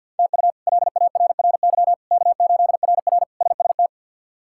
Morse code, heard as a keyed tone, about 52 wpm, 700 Hz. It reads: TU LARRY K8RR SST
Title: K1USN SST QSO Practice - 50wpm